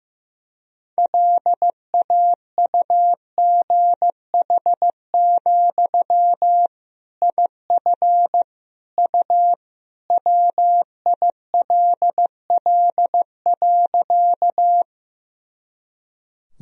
Morse code, heard as a keyed tone, about 16 wpm, 700 Hz. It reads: LAUGH, IF U WILL.
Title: Sentences from Top 500 Words and Abbreviations 15wpm